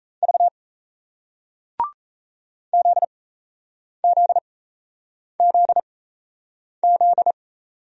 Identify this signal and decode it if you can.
Morse code, heard as a keyed tone, about 38 wpm, spaced as for 8 wpm, 700 Hz. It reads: V ZZZZ